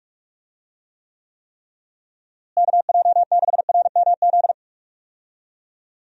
Morse code, heard as a keyed tone, about 44 wpm, 700 Hz. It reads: KJ6RMZ
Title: USA Call Signs - 6 Characters - Repeated 3x - 45wpm